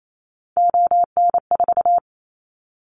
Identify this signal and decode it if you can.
Morse code, heard as a keyed tone, about 28 wpm, 700 Hz. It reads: ON4
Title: Callsign Prefixes for Top 100 DXCC - 28wpm